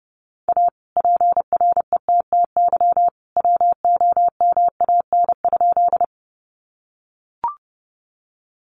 Morse code, heard as a keyed tone, about 29 wpm, 700 Hz. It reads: A PRETTY WOMAN?
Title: Sentences from Top 1100 Words 30wpm